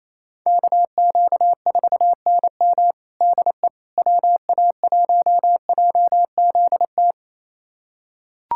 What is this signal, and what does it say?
Morse code, heard as a keyed tone, about 28 wpm, 700 Hz. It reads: KQ4NM DE WA1JZT
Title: USA Callsign DE Callsign 28wpm